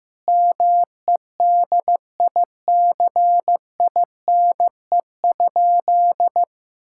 Morse code, heard as a keyed tone, about 15 wpm, 700 Hz. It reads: MEDICINE?
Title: Sentences from Top 1000 Words - Review Part 6 - 15wpm